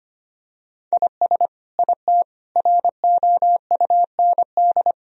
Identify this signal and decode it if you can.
Morse code, heard as a keyed tone, about 25 wpm, 700 Hz. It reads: IS IT ROUND
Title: Sentences from Top 1000 Words - Review Part 3 - 25wpm